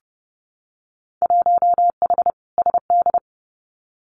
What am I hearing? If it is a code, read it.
Morse code, heard as a keyed tone, about 30 wpm, 700 Hz. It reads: 1H SD
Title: Winter Field Day Exchanges 30wpm